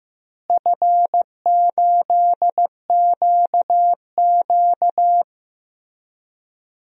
Morse code, heard as a keyed tone, about 15 wpm, 700 Hz. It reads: F8QQ
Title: DXCC-100 Call Signs - 4 Characters - Repeated 3x - 15wpm